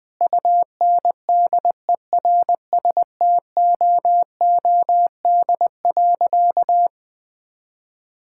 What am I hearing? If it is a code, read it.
Morse code, heard as a keyed tone, about 20 wpm, 700 Hz. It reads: UNDERSTOOD.
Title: Sentences with 2-Word Phrases - 20wpm